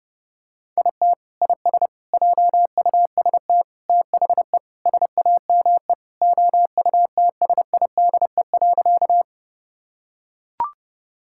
Morse code, heard as a keyed tone, about 31 wpm, 700 Hz. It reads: IT IS JUST THE SAME OUTSIDE.
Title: Sentences from Top 1000 Words - Review Part 6 - 30wpm